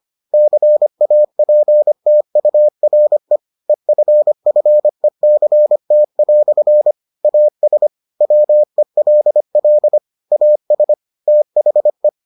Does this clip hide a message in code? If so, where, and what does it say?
Yes: Morse code from 0.3 to 12.1 s, heard as a keyed tone, about 25 wpm, 600 Hz. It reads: CAPTURE EFFECT" AS WELL AS THE